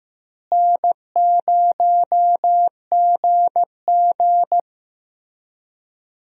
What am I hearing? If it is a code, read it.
Morse code, heard as a keyed tone, about 15 wpm, 700 Hz. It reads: N0GG
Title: USA Call Signs - 4 Characters - Repeated 3x - 15wpm